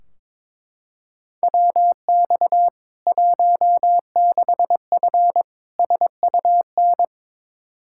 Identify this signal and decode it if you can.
Morse code, heard as a keyed tone, about 22 wpm, 700 Hz. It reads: WX 16F SUN
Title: K1USN SST QSO Practice 22wpm